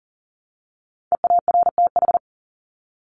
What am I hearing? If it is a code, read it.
Morse code, heard as a keyed tone, about 40 wpm, 700 Hz. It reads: EARTH
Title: Top 300 Words Repeated 40wpm